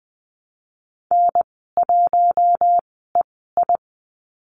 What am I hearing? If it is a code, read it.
Morse code, heard as a keyed tone, about 20 wpm, spaced as for 15 wpm, 700 Hz. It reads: N1EI
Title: 042 Lesson 8g - Number 1 - Call Signs 20wpm